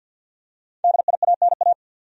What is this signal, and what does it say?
Morse code, heard as a keyed tone, about 49 wpm, 700 Hz. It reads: DIANA
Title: USA Common Names - 50wpm